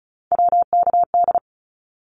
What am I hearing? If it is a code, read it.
Morse code, heard as a keyed tone, about 36 wpm, 700 Hz. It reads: WKD
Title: Abbreviations 35wpm